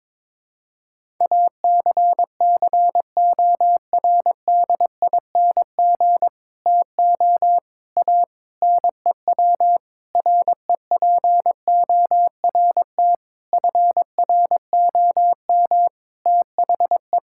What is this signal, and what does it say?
Morse code, heard as a keyed tone, about 21 wpm, 700 Hz. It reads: ACCORDING TO A NEW REPORT FROM THE
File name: Sentences with 7-Word Phrases - 22wpm